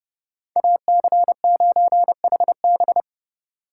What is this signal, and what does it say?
Morse code, heard as a keyed tone, about 29 wpm, 700 Hz. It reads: AC9HB